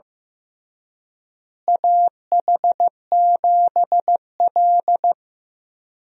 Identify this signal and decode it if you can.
Morse code, heard as a keyed tone, about 15 wpm, 700 Hz. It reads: AH7L
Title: USA Call Signs - 4 Characters - Repeated 3x - 15wpm